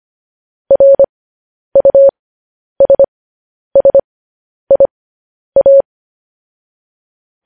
Morse code, heard as a keyed tone, about 25 wpm, spaced as for 10 wpm, 550 Hz. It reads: RUSSIA
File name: podcast / News headlines-1x-25x10